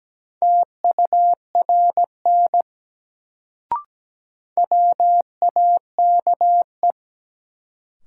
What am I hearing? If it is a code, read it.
Morse code, heard as a keyed tone, about 17 wpm, 700 Hz. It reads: TURN WAKE
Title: Sets of 4 Words - New Word Drill - 17wpm